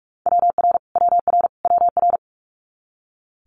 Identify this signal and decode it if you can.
Morse code, heard as a keyed tone, about 46 wpm, 700 Hz. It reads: WR WR WR
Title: Multi-letter Phonemes Repeated 3x 45wpm